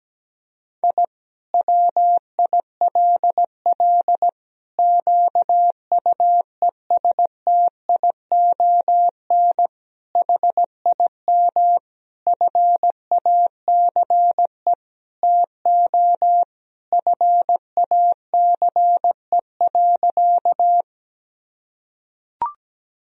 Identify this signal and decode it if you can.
Morse code, heard as a keyed tone, about 17 wpm, 700 Hz. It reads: I WILL QUESTION HIM FACE TO FACE.
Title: Sentences from Top 1000 Words - Review Part 2 - 17wpm